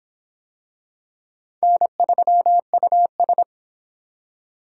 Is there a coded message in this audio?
Yes: Morse code from 1.6 to 3.4 s, heard as a keyed tone, about 25 wpm, 700 Hz. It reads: N3US